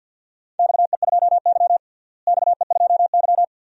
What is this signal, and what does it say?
Morse code, heard as a keyed tone, about 50 wpm, 700 Hz. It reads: XE2Y XE2Y
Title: DXCC-100 Call Signs - 4 Characters - Repeated 2x - 50wpm